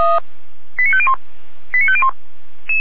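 Five-tone selective call (ZVEI), heard as quick runs of five tones, tones of about 70 ms. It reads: 89501 89501
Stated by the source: Five-tone ZVEI selective call received by SDR, archived by feuerwehr-orlamuende.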